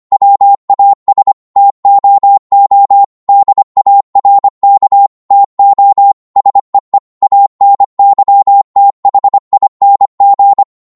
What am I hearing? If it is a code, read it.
Morse code, heard as a keyed tone, about 25 wpm, 800 Hz. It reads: WAS TOO DARK TO SEE ANYTHING